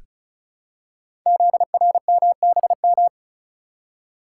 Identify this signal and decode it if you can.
Morse code, heard as a keyed tone, about 36 wpm, 700 Hz. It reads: ZRMBM
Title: Code Groups - 5 Letters - 35wpm